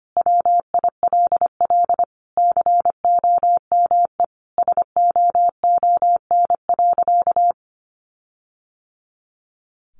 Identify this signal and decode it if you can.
Morse code, heard as a keyed tone, about 25 wpm, 700 Hz. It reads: WILL COME SOON.
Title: Sentences from Top 1500 Words 25wpm